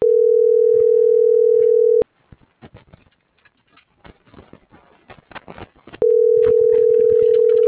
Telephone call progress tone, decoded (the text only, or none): ringback tone